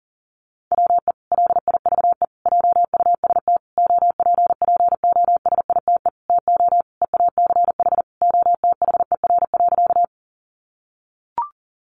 Morse code, heard as a keyed tone, about 37 wpm, 700 Hz. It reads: WE LIVE JUST OPPOSITE TO EACH OTHER.
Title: Sentences from Top 1300 Words 40wpm